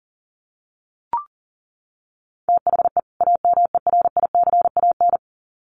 Morse code, heard as a keyed tone, about 40 wpm, 700 Hz. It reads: THE AMERICAN
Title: Sets of 2 Words 40wpm